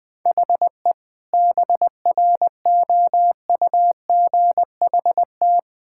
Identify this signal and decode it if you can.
Morse code, heard as a keyed tone, about 20 wpm, 700 Hz. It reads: HE BROUGHT
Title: Sentences with 6-Word Phrases - Spoken Beginning - 20wpm